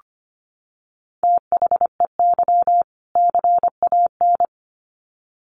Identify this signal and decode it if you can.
Morse code, heard as a keyed tone, about 25 wpm, 700 Hz. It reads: THEY CAN
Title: Sets of 2 Words 25wpm